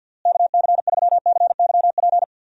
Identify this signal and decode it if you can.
Morse code, heard as a keyed tone, about 46 wpm, 700 Hz. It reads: KC3CYP
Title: USA Call Signs - 6 Characters - Repeated 3x - 50wpm